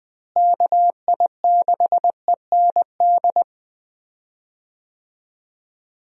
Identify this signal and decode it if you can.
Morse code, heard as a keyed tone, about 21 wpm, 700 Hz. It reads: KI6END
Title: USA Call Signs - 6 Characters - Repeated 3x - 20wpm